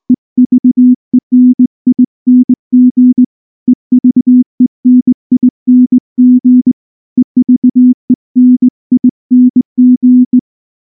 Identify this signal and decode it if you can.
Morse code, heard as a keyed tone, about 19 wpm, 250 Hz. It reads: EVENING EVENING EVENING
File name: morse / morse-training-20WPM-6